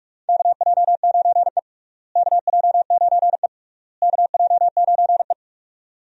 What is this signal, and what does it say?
Morse code, heard as a keyed tone, about 41 wpm, 700 Hz. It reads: KJ9E KJ9E KJ9E
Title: USA Call Signs - 4 Characters - Repeated 3x - 45wpm